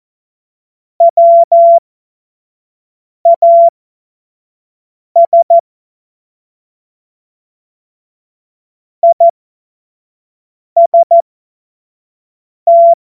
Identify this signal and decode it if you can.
Morse code, heard as a keyed tone, about 14 wpm, spaced as for 5 wpm, 680 Hz. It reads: WAS IST
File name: Witz-14-Oktober-2025-25-F70